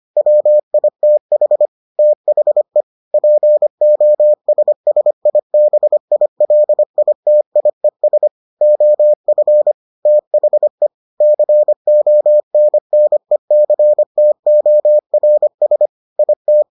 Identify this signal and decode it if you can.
Morse code, heard as a keyed tone, about 25 wpm, 600 Hz. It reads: WITH THE POSSIBILITIES OF THE CONNECTORS IT